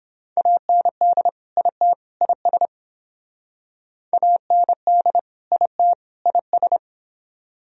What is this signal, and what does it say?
Morse code, heard as a keyed tone, about 26 wpm, 700 Hz. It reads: AND IT IS AND IT IS
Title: Speed-Racing - Sets of 3 Words 22wpm